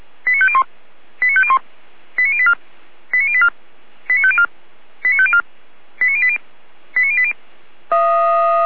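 Five-tone selective call (ZVEI), heard as quick runs of five tones, tones of about 70 ms. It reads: 89501 89501 89084 89084 89504 89504 89080 89080
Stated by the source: ZVEI five-tone selective call received with an SDR.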